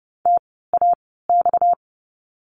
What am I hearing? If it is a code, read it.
Morse code, heard as a keyed tone, about 30 wpm, spaced as for 17 wpm, 700 Hz. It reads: TAX
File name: course / 30:10wpm / 230 Lesson 37f - Letter x - Words 30wpm